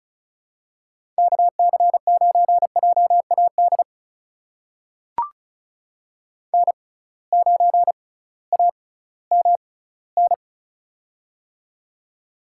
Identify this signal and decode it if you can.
Morse code, heard as a keyed tone, about 36 wpm, spaced as for 19 wpm, 700 Hz. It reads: KC9JAD N9AMN
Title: USA Call Signs - 6x Character Spacing - 35wpm